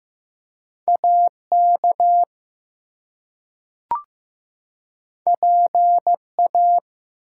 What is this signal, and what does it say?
Morse code, heard as a keyed tone, about 15 wpm, 700 Hz. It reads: AK PA
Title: US States and Canadian Provinces 15wpm